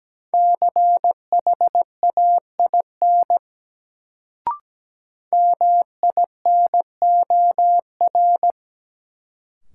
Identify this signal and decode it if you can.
Morse code, heard as a keyed tone, about 17 wpm, 700 Hz. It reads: CHAIN MINOR
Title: Sentences from Top 1500 Words - New Word Drill - 17wpm